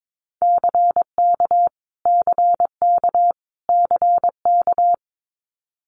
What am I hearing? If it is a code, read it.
Morse code, heard as a keyed tone, about 22 wpm, 700 Hz. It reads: CK CK CK
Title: Multi-letter Phonemes Repeated 3x 22wpm